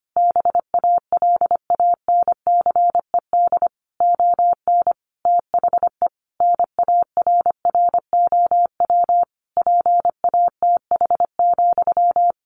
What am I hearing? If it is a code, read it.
Morse code, heard as a keyed tone, about 25 wpm, 700 Hz. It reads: BALANCED ON THE NARROW PATH,
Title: A Sound Of Thunder 25wpm